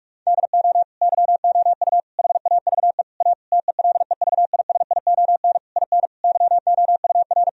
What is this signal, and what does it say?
Morse code, heard as a keyed tone, about 41 wpm, 700 Hz. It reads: DO YOU HAVE A TELEVISION IN YOUR
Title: Sentences from Top 1000 Words - Review Part 4 - 45wpm